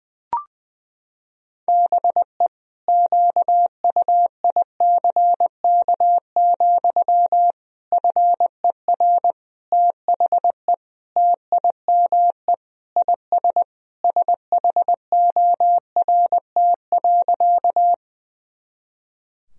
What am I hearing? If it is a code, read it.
Morse code, heard as a keyed tone, about 20 wpm, 700 Hz. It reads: BE QUICK, FER THE TIME IS SHORT.
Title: Sentences from Top 500 Words and Abbreviations 20wpm